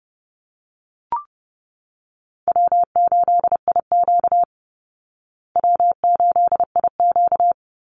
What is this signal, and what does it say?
Morse code, heard as a keyed tone, about 30 wpm, 700 Hz. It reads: W8IQ W8IQ
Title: Call Signs Repeated 2x - 30wpm